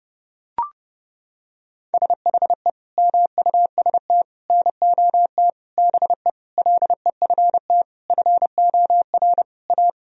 Morse code, heard as a keyed tone, about 32 wpm, 700 Hz. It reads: SHE MUST NOT BE LEFT FOR A
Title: Sentences from Top 1000 Words - Review Part 1 - 30wpm